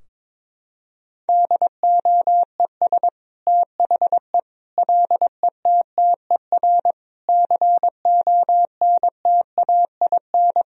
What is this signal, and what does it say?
Morse code, heard as a keyed tone, about 22 wpm, 700 Hz. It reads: DOES THE LETTER CONTAIN
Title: Sentences from Top 1000 Words - Review Part 2 - 22wpm